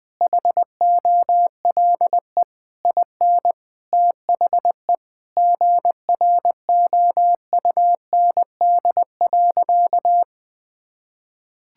Morse code, heard as a keyed tone, about 21 wpm, 700 Hz. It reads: HOLE IN THE GROUND.
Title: Sentences from Top 1600 Words 20wpm